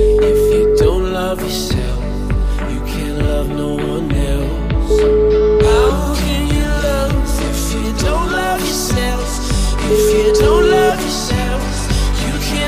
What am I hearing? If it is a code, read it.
Ringback tone (Europe, 425 Hz), with noise.